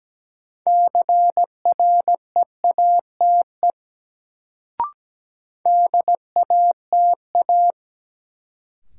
Morse code, heard as a keyed tone, about 17 wpm, 700 Hz. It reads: CREATE DATA
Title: Sentences from Top 1000 Words - Review Part 3 - New Word Drill - 17wpm